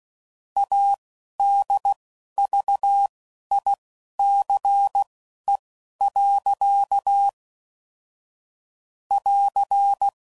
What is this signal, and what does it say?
Morse code, heard as a keyed tone, about 16 wpm, spaced as for 11 wpm, 800 Hz. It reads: ADVICE. <AR>